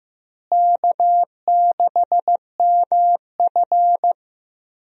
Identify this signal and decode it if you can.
Morse code, heard as a keyed tone, about 16 wpm, 700 Hz. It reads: K6MF
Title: USA Call Signs - 4 Characters - Repeated 3x - 15wpm